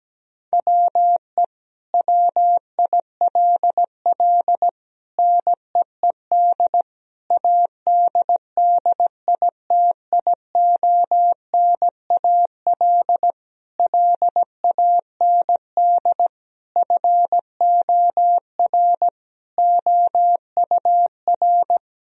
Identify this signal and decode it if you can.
Morse code, heard as a keyed tone, about 17 wpm, 700 Hz. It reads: WE WILL NEED ADDITIONAL LAND FOR OUR
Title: Sentences from Top 1000 Words - Review Part 4 - 17wpm